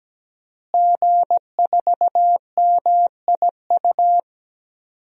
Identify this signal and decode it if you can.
Morse code, heard as a keyed tone, about 17 wpm, 700 Hz. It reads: G4MIU
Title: DXCC-100 Call Signs - 5 Characters - Repeated 2x - 17wpm